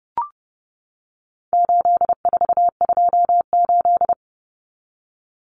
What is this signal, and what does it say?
Morse code, heard as a keyed tone, about 30 wpm, 700 Hz. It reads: 8428
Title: Numbers - 4-Digit 30wpm